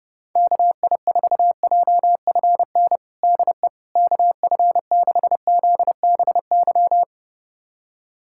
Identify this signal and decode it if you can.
Morse code, heard as a keyed tone, about 30 wpm, 700 Hz. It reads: KI4JFN DE KF6ZBY
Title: USA Callsign DE Callsign - 6 Characters - 30wpm